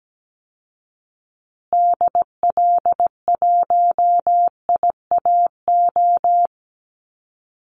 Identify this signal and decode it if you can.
Morse code, heard as a keyed tone, about 17 wpm, 700 Hz. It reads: DL1IAO
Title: Call Signs Repeated 2x - 17wpm